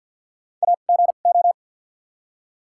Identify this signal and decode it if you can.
Morse code, heard as a keyed tone, about 54 wpm, spaced as for 37 wpm, 700 Hz. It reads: AGO
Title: 2x Character Spacing - Top 500 Words - 50wpm